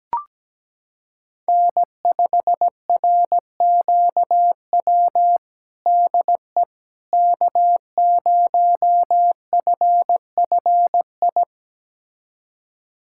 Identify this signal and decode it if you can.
Morse code, heard as a keyed tone, about 17 wpm, 700 Hz. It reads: N5RQW DE K0FFI
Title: USA Callsign DE Callsign 17wpm